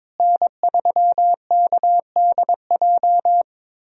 Morse code, heard as a keyed tone, about 22 wpm, 700 Hz. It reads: N3KDJ
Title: USA Call Signs - 5 Characters - Repeated 2x - 22wpm